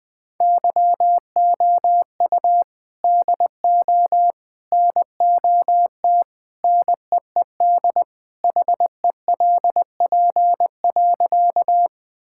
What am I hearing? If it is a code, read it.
Morse code, heard as a keyed tone, about 20 wpm, 700 Hz. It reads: YOU DO NOT NEED HELP.
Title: Sentences with 3-Word Phrases - Spoken Beginning - 20wpm